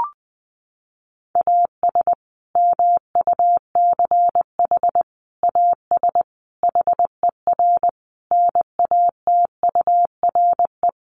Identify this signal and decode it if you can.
Morse code, heard as a keyed tone, about 20 wpm, 700 Hz. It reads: AS MUCH AS HER NATURE
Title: Sentences from Top 700 Words 20wpm